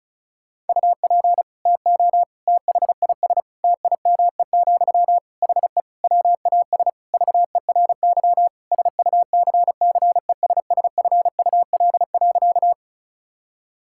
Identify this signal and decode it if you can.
Morse code, heard as a keyed tone, about 36 wpm, 700 Hz. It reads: UP TO THIS TIME, HE WAS VERY SUCCESSFUL.
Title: Sentences from Top 1000 Words - Review Part 6 - 35wpm